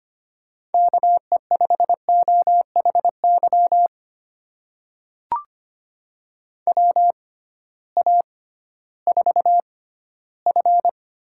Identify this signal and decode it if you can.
Morse code, heard as a keyed tone, about 25 wpm, spaced as for 15 wpm, 700 Hz. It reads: KE5OHY WA4F
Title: USA Call Signs - 6 Characters - 6x Character Spacing - 25wpm